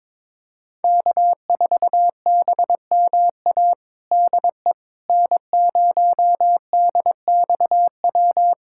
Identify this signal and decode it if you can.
Morse code, heard as a keyed tone, about 21 wpm, 700 Hz. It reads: K4BMA DE N0DXW
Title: USA Callsign DE Callsign - 5 Characters - 22wpm